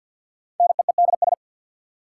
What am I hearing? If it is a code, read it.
Morse code, heard as a keyed tone, about 46 wpm, 700 Hz. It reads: NEEDS
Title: Sets of 3 Words - New Word Drill - 50wpm